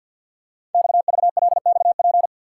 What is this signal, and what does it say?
Morse code, heard as a keyed tone, about 50 wpm, 690 Hz. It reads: XVLXP
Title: Code Groups - 5 Letters - 50wpm